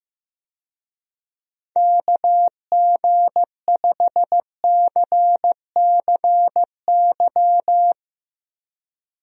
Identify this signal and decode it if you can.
Morse code, heard as a keyed tone, about 15 wpm, 700 Hz. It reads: KG5CCY
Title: USA Call Signs - 6 Characters - Repeated 2x - 15wpm